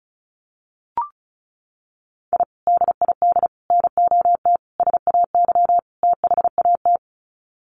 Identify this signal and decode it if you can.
Morse code, heard as a keyed tone, about 34 wpm, 700 Hz. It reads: I DID NOT SAY THAT
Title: Sets of 5 Words Encore2 35wpm